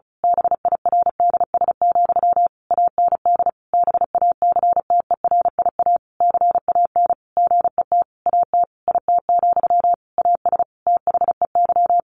Morse code, heard as a keyed tone, about 34 wpm, 700 Hz. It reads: BIRDS, AND BACTERIA CAN GET AT IT, AS THEY
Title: A Sound Of Thunder 35wpm